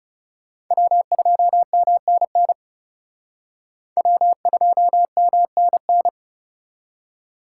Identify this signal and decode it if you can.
Morse code, heard as a keyed tone, about 37 wpm, 700 Hz. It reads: W2MNN W2MNN
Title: Speed Racing - USA Call Signs 30wpm